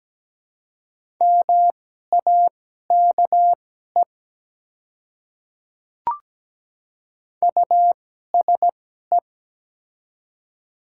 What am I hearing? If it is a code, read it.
Morse code, heard as a keyed tone, about 17 wpm, spaced as for 12 wpm, 700 Hz. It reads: MAKE USE
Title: 2x Character Spacing - Top 100 Words - 17wpm